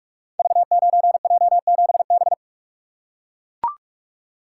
Morse code, heard as a keyed tone, about 44 wpm, 700 Hz. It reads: V9J7B
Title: Code Groups - 5 Letter:Numbers - 45wpm